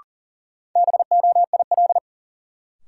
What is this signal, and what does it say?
Morse code, heard as a keyed tone, about 40 wpm, 700 Hz. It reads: BOIL
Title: Sets of 5 Words Encore - New Word Drill - 40wpm